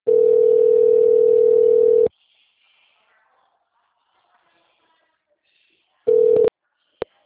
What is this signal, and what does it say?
Ringback tone (North America), 440 + 480 Hz.